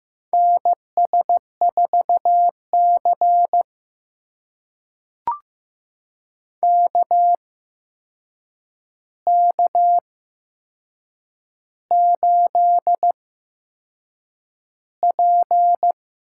Morse code, heard as a keyed tone, about 15 wpm, spaced as for 6 wpm, 700 Hz. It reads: NS4C KK8P
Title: USA Call Signs - 4 Characters - 8x Character Spacing - 15wpm